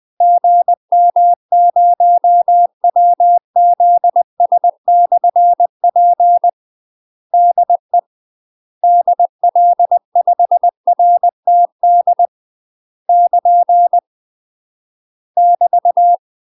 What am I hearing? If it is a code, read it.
Morse code, heard as a keyed tone, about 20 wpm, 700 Hz. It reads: GM0WZS/P DE DL5RTD <KN> <BT>
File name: UK MASTER QSOs TRAINING / 20WPM / UK MASTER QSOs 0045